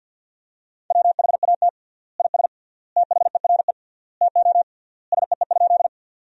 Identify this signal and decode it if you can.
Morse code, heard as a keyed tone, about 50 wpm, 700 Hz. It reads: WHAT IS THERE TO SEE?
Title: 3x Word-Spacing - Sentences from Top 100 Words - 50wpm